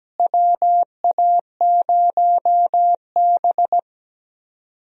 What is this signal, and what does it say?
Morse code, heard as a keyed tone, about 17 wpm, 700 Hz. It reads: WA0B
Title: USA Call Signs - 4 Characters - Repeated 3x - 17wpm